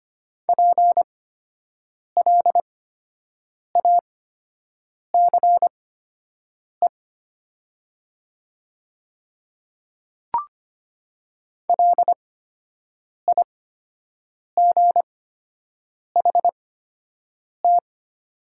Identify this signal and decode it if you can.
Morse code, heard as a keyed tone, about 25 wpm, spaced as for 7 wpm, 700 Hz. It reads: PLACE LIGHT
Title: 8x Character Spacing - Top 200 Words - 25wpm